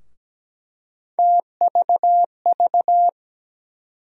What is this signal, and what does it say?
Morse code, heard as a keyed tone, about 17 wpm, 700 Hz. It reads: TVV
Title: USA Call Sign Postfixes - 3 Letters 17wpm